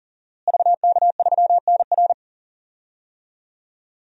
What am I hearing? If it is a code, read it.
Morse code, heard as a keyed tone, about 40 wpm, 700 Hz. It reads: VK3NR
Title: DXCC-100 Call Signs - 5 Characters - Repeated 3x - 40wpm